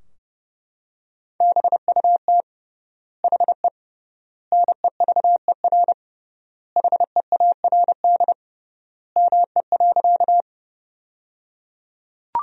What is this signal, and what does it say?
Morse code, heard as a keyed tone, about 30 wpm, 700 Hz. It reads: BUT HE NEVER HEARD ME.
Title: 3x Word-Spacing - Sentences from Top 400 Words - 30wpm